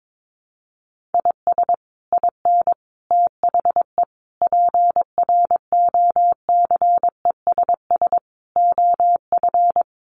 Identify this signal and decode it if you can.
Morse code, heard as a keyed tone, about 22 wpm, 700 Hz. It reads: IS IN THE PROCESS OF
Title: Sets of 5 Words Encore2 22wpm